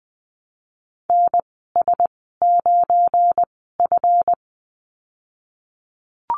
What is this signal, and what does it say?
Morse code, heard as a keyed tone, about 20 wpm, spaced as for 15 wpm, 700 Hz. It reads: NS9F
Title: 186 Lesson 30g - Letter b - Call Signs 20wpm